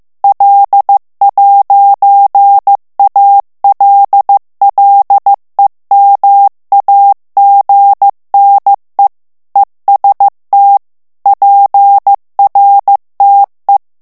Morse code, heard as a keyed tone, about 15 wpm, 800 Hz. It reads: L'ALLEMAGNE EST PRTE
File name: morse mp3 / FRANCAIS TEXTE CLAIR 1 VITESSE 14 MOTS MINUTES